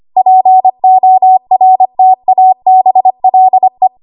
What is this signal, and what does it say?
Morse code, heard as a keyed tone, about 25 wpm, 750 Hz. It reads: PORTABLE
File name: Morse / 25WPM